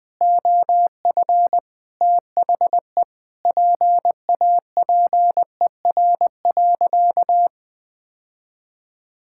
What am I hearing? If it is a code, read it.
Morse code, heard as a keyed tone, about 21 wpm, 700 Hz. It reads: OF THE PAPER.
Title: Sentences with 5-Word Phrases - 20wpm